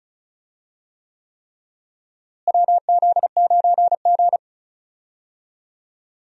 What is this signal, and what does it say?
Morse code, heard as a keyed tone, about 34 wpm, 700 Hz. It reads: WZ9G
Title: USA Call Signs - 4 Characters - Repeated 3x - 35wpm